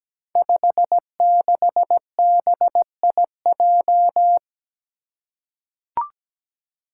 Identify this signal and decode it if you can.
Morse code, heard as a keyed tone, about 17 wpm, 700 Hz. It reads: 56BIJ
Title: Five Challenging Letter-Number Combinations 17wpm